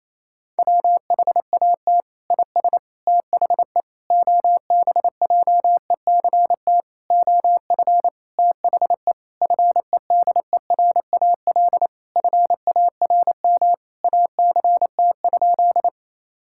Morse code, heard as a keyed tone, about 27 wpm, 700 Hz. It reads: WHAT IS THE OBJECT OF THE FEDERAL FARM ACT?
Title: Sentences from Top 1000 Words - Review Part 5 - 28wpm